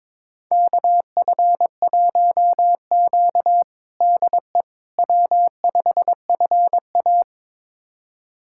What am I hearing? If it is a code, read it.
Morse code, heard as a keyed tone, about 22 wpm, 700 Hz. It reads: KF1Q DE W5FA
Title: USA Callsign DE Callsign - 4 Characters - 22wpm